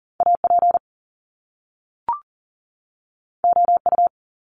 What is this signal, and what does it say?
Morse code, heard as a keyed tone, about 40 wpm, 700 Hz. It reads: AP OU
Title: Top Two Letter Combinations in Words 40wpm